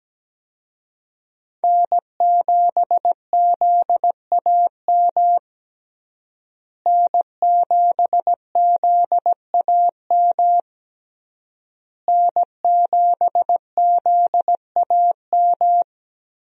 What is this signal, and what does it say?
Morse code, heard as a keyed tone, about 17 wpm, 700 Hz. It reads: N7ZAM N7ZAM N7ZAM
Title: USA Call Signs - 5 Characters - Repeated 3x - 17wpm